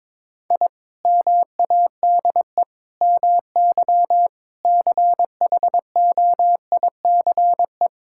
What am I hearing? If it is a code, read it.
Morse code, heard as a keyed tone, about 21 wpm, 700 Hz. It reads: I MADE MY CHOICE
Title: Sentences from Top 1000 Words - Review Part 6 - 22wpm